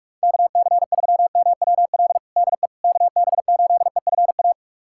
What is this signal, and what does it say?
Morse code, heard as a keyed tone, about 46 wpm, 700 Hz. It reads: KC3MWL DE KB8EFA
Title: USA Callsign DE Callsign - 6 Characters - 45wpm